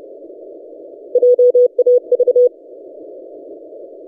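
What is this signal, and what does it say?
Morse code, heard as a keyed tone, about 30 wpm, 490 Hz. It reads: JAV
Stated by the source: DX station 3D2GC on 20 metres CW, recorded by N6ML.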